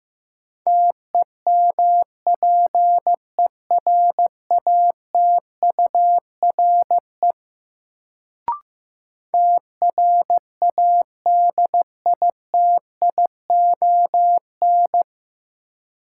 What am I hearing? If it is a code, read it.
Morse code, heard as a keyed tone, about 15 wpm, 700 Hz. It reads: TEMPERATURE TRADITION
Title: Sentences from Top 1500 Words - New Word Drill Alt - 15wpm